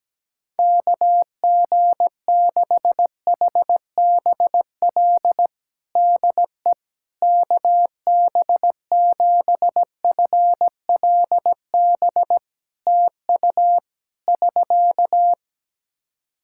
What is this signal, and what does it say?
Morse code, heard as a keyed tone, about 17 wpm, 700 Hz. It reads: KG6HBL DE KB7FLB TU <SK>
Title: Practice QSO - Exchange 4 - Ending - 17wpm